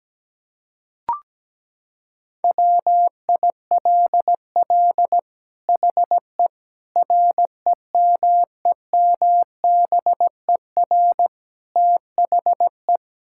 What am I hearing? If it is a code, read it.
Morse code, heard as a keyed tone, about 17 wpm, 700 Hz. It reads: WILL HE REMEMBER THE
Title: Sentences from Top 1000 Words - Review Part 6 - 17wpm